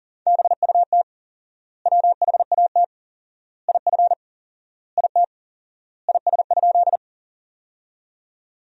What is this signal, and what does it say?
Morse code, heard as a keyed tone, about 38 wpm, 700 Hz. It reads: BUT WHAT IF IT IS?
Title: 4x Word-Spacing - Sentences from Top 100 Words - 40wpm